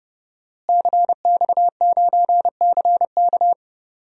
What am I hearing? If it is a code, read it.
Morse code, heard as a keyed tone, about 30 wpm, 700 Hz. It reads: CX9CK
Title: DXCC-100 Call Signs - 5 Characters - Repeated 2x - 30wpm